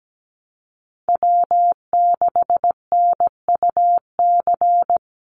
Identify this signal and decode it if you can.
Morse code, heard as a keyed tone, about 17 wpm, 700 Hz. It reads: W6NUC
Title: Call Signs Repeated 2x - 17wpm